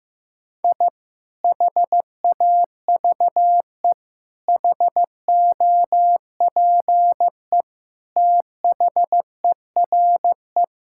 Morse code, heard as a keyed tone, about 15 wpm, 700 Hz. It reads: I HAVE HOPE THERE
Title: Sentences from Top 1000 Words - Review Part 5 - 15wpm